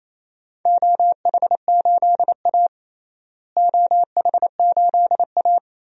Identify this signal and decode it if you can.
Morse code, heard as a keyed tone, about 28 wpm, 700 Hz. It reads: OH8A OH8A
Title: DXCC-100 Call Signs - 4 Characters - Repeated 2x - 28wpm